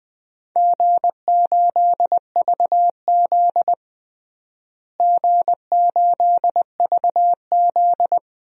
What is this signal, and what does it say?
Morse code, heard as a keyed tone, about 21 wpm, 700 Hz. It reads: G8VZ G8VZ
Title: DXCC-100 Call Signs - 4 Characters - Repeated 3x - 20wpm